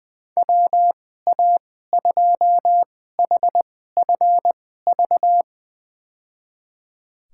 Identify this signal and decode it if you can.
Morse code, heard as a keyed tone, about 20 wpm, spaced as for 15 wpm, 700 Hz. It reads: WA2HFV
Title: USA Call Signs - 2x Character Spacing - 20wpm